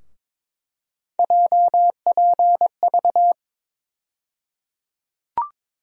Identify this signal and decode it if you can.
Morse code, heard as a keyed tone, about 22 wpm, 700 Hz. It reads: JPV
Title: USA Call Sign Postfixes - 3 Letters 22wpm